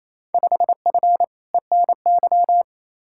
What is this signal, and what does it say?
Morse code, heard as a keyed tone, about 28 wpm, 700 Hz. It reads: 5F ENY
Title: Field Day Exchanges Encore - 28wpm